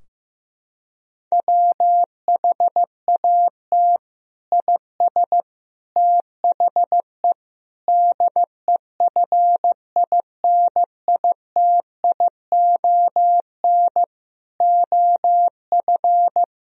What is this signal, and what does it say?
Morse code, heard as a keyed tone, about 15 wpm, 700 Hz. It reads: WHAT IS THE DEFINITION OF
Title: Sentences from Top 1000 Words - Review Part 4 - 15wpm